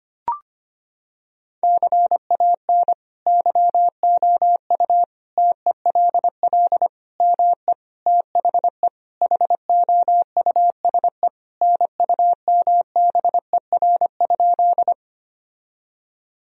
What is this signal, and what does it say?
Morse code, heard as a keyed tone, about 25 wpm, 700 Hz. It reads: CAN YOU TELL ME THE HOUSE NUMBER?
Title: Sentences from Top 1000 Words - Review Part 1 - 25wpm